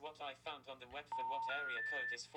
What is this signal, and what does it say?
Special information tone, with speech.